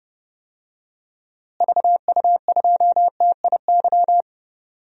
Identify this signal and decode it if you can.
Morse code, heard as a keyed tone, about 31 wpm, 700 Hz. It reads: VU2TIY
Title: DXCC-100 Call Signs - 6 Characters - Repeated 3x - 30wpm